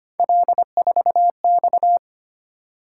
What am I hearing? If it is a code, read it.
Morse code, heard as a keyed tone, about 25 wpm, 700 Hz. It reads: L4X
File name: Three Letter-Number Combinations - 25wpm